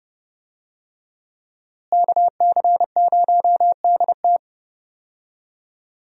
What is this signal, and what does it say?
Morse code, heard as a keyed tone, about 31 wpm, 700 Hz. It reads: KC0DT
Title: USA Call Signs - 5 Characters - Repeated 3x - 30wpm